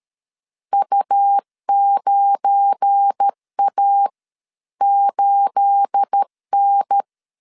Morse code, heard as a keyed tone, about 13 wpm, spaced as for 11 wpm, 800 Hz. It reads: U9A8N
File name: CW / DET / 1 9 1500C S C 12wpm 3811